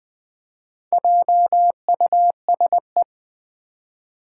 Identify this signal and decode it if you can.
Morse code, heard as a keyed tone, about 20 wpm, 700 Hz. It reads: JUSE
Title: Four Challenging Letter-Number Combinations 20wpm